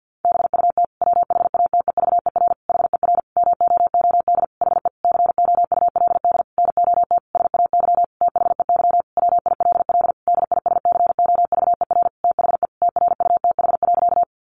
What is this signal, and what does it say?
Morse code, heard as a keyed tone, about 50 wpm, 700 Hz. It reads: BUT WHATEVER HER MOOD HE COULD NOT SAY THEY WILL DISCOVER THE TRUTH.
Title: Sentences from Top 1100 Words 50wpm